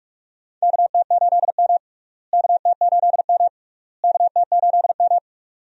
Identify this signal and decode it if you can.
Morse code, heard as a keyed tone, about 44 wpm, 700 Hz. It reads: KT8M KT8M KT8M
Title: USA Call Signs - 4 Characters - Repeated 3x - 45wpm